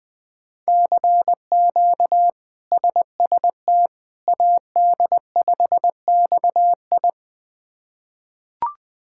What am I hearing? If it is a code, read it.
Morse code, heard as a keyed tone, about 20 wpm, 700 Hz. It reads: CQ SST AD5XI
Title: K1USN SST QSO Practice - 20wpm